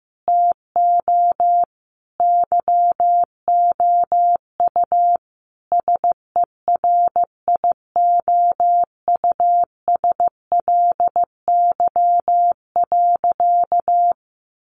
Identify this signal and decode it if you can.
Morse code, heard as a keyed tone, about 15 wpm, 700 Hz. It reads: TO YOU SERIOUSLY.
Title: Sentences from Top 1200 Words 15wpm